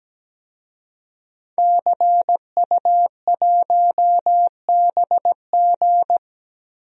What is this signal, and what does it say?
Morse code, heard as a keyed tone, about 17 wpm, 700 Hz. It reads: CU1BG
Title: DXCC-100 Call Signs - 5 Characters - Repeated 2x - 17wpm